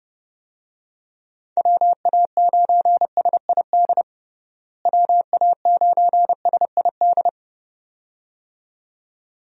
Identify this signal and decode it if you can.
Morse code, heard as a keyed tone, about 30 wpm, 700 Hz. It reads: WA9SID WA9SID
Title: USA Call Signs - 6 Characters - Repeated 2x - 30wpm